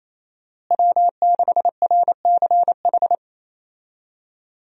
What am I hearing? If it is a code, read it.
Morse code, heard as a keyed tone, about 28 wpm, 700 Hz. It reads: W6RCH
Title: USA Call Signs - 5 Characters - Repeated 2x - 28wpm